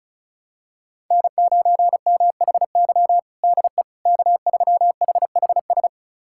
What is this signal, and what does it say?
Morse code, heard as a keyed tone, about 34 wpm, 700 Hz. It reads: N9MHY DE K3HHS
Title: USA Callsign DE Callsign - 5 Characters - 35wpm